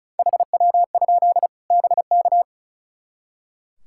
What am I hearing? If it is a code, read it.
Morse code, heard as a keyed tone, about 36 wpm, 700 Hz. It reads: HW? BK
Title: Practice QSO - Exchange 1 - Beginning - 35wpm